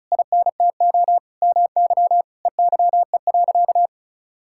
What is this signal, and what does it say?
Morse code, heard as a keyed tone, about 36 wpm, 700 Hz. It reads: INTO MY EYE.
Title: Sentences from Top 1600 Words 35wpm